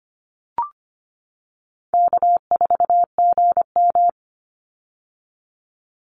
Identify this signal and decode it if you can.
Morse code, heard as a keyed tone, about 25 wpm, 700 Hz. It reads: K4GM
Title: Call Signs - 25wpm